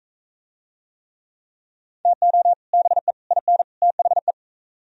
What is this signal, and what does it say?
Morse code, heard as a keyed tone, about 40 wpm, 700 Hz. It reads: TO BE IN THE